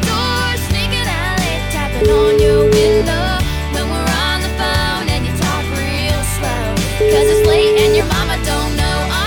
Ringback tone (Europe, 425 Hz), with noise.